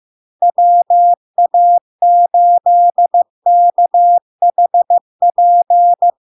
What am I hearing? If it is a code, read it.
Morse code, heard as a keyed tone, about 15 wpm, 700 Hz. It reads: WA8KHP
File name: LARK-FD-2022-Log-in-CW-15wpm-700-Hz